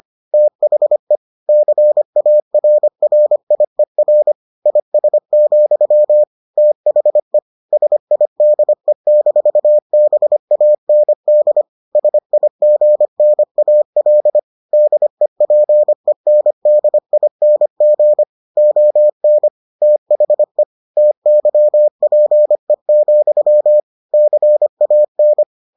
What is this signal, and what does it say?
Morse code, heard as a keyed tone, about 25 wpm, 600 Hz. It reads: THE CARRIER IS, THE SIDE-BAND SIGNAL DEPENDING ON THE TYPE, CAN